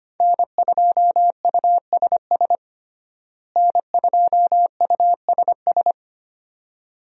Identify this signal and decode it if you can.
Morse code, heard as a keyed tone, about 25 wpm, 700 Hz. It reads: N2USS N2USS
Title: USA Call Signs - 5 Characters - Repeated 2x - 25wpm